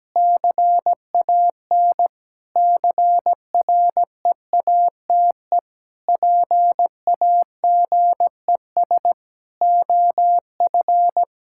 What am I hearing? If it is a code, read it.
Morse code, heard as a keyed tone, about 17 wpm, 700 Hz. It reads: CAN CREATE PAGES OF
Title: Pangram Challenge v2 - 17wpm